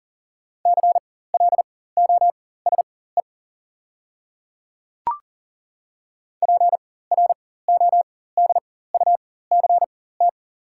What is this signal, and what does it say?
Morse code, heard as a keyed tone, about 40 wpm, spaced as for 19 wpm, 690 Hz. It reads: CLOSE PRODUCT
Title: 4x Character Spacing - Top 500 Words - 40wpm